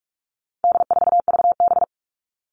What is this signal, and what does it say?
Morse code, heard as a keyed tone, about 44 wpm, 700 Hz. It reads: D4VB
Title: All Cut Numbers - 4-Digit 45wpm